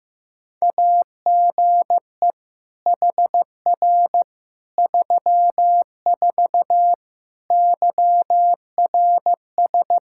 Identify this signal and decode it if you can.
Morse code, heard as a keyed tone, about 15 wpm, 700 Hz. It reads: AGE HR 34 YRS
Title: K1USN SST QSO Practice 15wpm